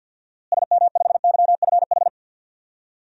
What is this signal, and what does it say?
Morse code, heard as a keyed tone, about 51 wpm, 700 Hz. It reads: SM5YFH
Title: DXCC-100 Call Signs - 6 Characters - Repeated 2x - 50wpm